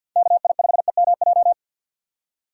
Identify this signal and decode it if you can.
Morse code, heard as a keyed tone, about 48 wpm, 700 Hz. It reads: KI5EMJ
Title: USA Call Signs - 6 Characters - 8x Character Spacing - 50wpm